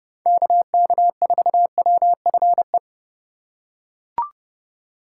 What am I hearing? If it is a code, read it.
Morse code, heard as a keyed tone, about 30 wpm, 700 Hz. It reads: KK4WFE
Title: USA Call Signs - 6 Characters - 3x Character Spacing - 30wpm